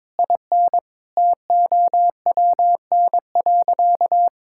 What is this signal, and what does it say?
Morse code, heard as a keyed tone, about 22 wpm, 700 Hz. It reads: IN TOWN.
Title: Sentences from Top 1600 Words 22wpm